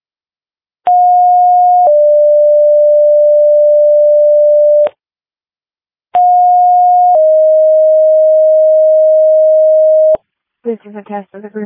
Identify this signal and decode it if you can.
Two-tone page, heard as a one-second tone, then a three-second tone, about 720 then 590 Hz, about 720 then 620 Hz, with speech. It is heard from 0.9 to 10.2 s.
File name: Janesville First Responders 2026 04 16 19 02 21